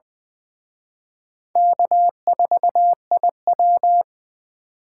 Morse code, heard as a keyed tone, about 23 wpm, 700 Hz. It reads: K4IW
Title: Speed Racing - USA 1x2 Call Signs 20wpm